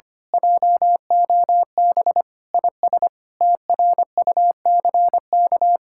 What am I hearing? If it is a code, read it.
Morse code, heard as a keyed tone, about 24 wpm, 700 Hz. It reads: JOB IS TRUCK
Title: Speed-Racing Common QSO Phrases Encore - 25wpm